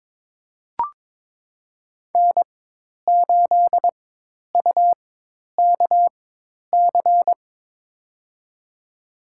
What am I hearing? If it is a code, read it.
Morse code, heard as a keyed tone, about 22 wpm, spaced as for 10 wpm, 700 Hz. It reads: N8UKC
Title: USA Call Signs - 5 Characters - 4x Character Spacing - 22wpm